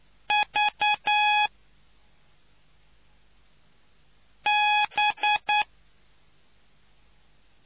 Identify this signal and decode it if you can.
Morse code, heard as a keyed tone, about 9 wpm, 840 Hz. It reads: V B